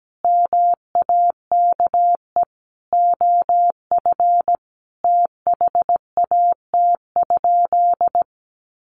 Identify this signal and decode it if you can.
Morse code, heard as a keyed tone, about 17 wpm, 700 Hz. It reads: MAKE OF THAT?
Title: Sentences from Top 100 Words 17wpm